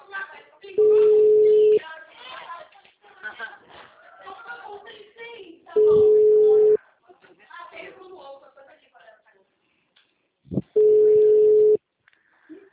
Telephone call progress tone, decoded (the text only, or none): ringback tone